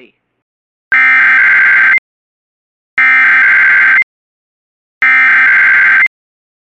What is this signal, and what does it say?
Emergency Alert System (SAME) tones (header bursts), heard as harsh buzzing bursts. It is heard from 0.9 to 6.0 s.
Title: Relaying Severe Thunderstorm Warning from KOD/DAS.